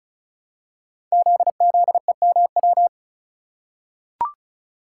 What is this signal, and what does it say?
Morse code, heard as a keyed tone, about 36 wpm, 690 Hz. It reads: ZZEMW
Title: Code Groups - 5 Letters - 35wpm